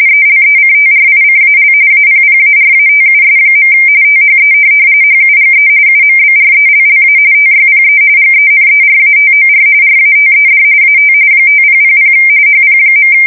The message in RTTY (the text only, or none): ELECTRICAL APPARATUS. THE HUGE UMBRELLA TYPE GENERATOR RATED AT 5000 HP.